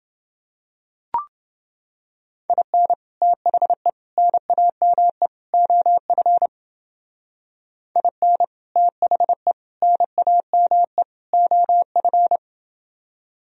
Morse code, heard as a keyed tone, about 28 wpm, 700 Hz. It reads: IN THE NAME OF IN THE NAME OF